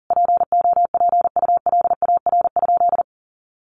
Morse code, heard as a keyed tone, about 40 wpm, 700 Hz. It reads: POPULAR?
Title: Sentences from Top 1200 Words 40wpm